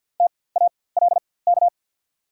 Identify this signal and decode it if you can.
Morse code, heard as a keyed tone, about 54 wpm, spaced as for 24 wpm, 700 Hz. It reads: TALK
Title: 4x Character Spacing - Top 500 Words - 50wpm